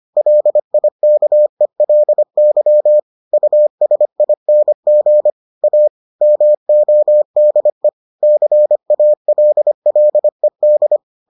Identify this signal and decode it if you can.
Morse code, heard as a keyed tone, about 25 wpm, 600 Hz. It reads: LIKELY USING A MODE CALLED